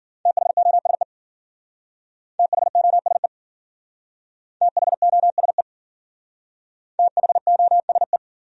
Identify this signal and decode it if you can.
Morse code, heard as a keyed tone, about 49 wpm, 700 Hz. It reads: THOSE THOSE THOSE THOSE
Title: Speed-Racing Top 500 Words 40wpm